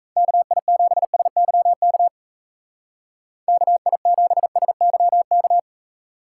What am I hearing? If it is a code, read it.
Morse code, heard as a keyed tone, about 36 wpm, 700 Hz. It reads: KI7SYK KI7SYK